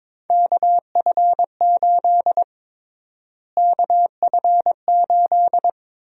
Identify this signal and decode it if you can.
Morse code, heard as a keyed tone, about 22 wpm, 700 Hz. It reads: KF8 KF8
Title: USA Call Sign Prefixes - Repeated 2x - 22wpm